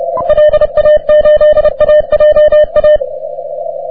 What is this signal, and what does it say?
Morse code, heard as a keyed tone, about 28 wpm, 600 Hz. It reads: LA8AJA
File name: wp / k5d 30m cw